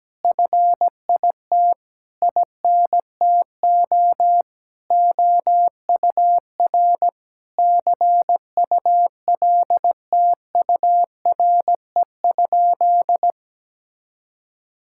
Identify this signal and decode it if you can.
Morse code, heard as a keyed tone, about 17 wpm, 700 Hz. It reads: FIT INTO OUR CULTURE?
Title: Sentences from Top 1000 Words - Review Part 6 - 17wpm